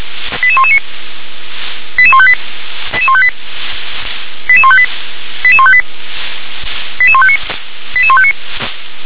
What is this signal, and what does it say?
Five-tone selective call (ZVEI), heard as quick runs of five tones, tones of about 70 ms. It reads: 88119 88158 88158 88158 88158 88159 88159
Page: SDR/ZVEI